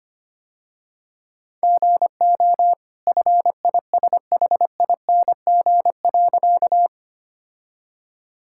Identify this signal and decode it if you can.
Morse code, heard as a keyed tone, about 25 wpm, 700 Hz. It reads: GO FISHING.
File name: Easy Sentences with Hints - 25wpm